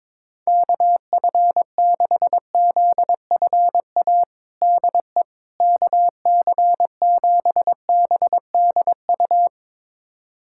Morse code, heard as a keyed tone, about 22 wpm, 700 Hz. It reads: KF6ZFA DE KC7BDU
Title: USA Callsign DE Callsign - 6 Characters - 22wpm